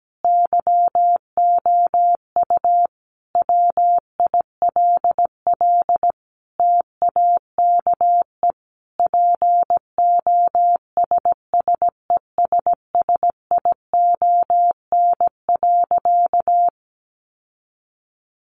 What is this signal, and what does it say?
Morse code, heard as a keyed tone, about 17 wpm, 700 Hz. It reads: YOU WILL TAKE POSSESSION.
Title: Sentences from Top 1400 Words 17wpm